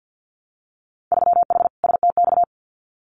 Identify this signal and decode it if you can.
Morse code, heard as a keyed tone, about 50 wpm, 700 Hz. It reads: 3H STX
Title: Winter Field Day Exchanges Encore 50wpm